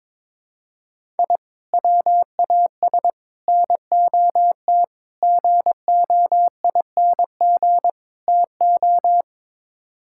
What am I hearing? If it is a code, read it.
Morse code, heard as a keyed tone, about 22 wpm, 700 Hz. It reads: I WAS NOT GOING TO
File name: practice / 22wpm / Build A Phrase - 5-Word Phrases - 22wpm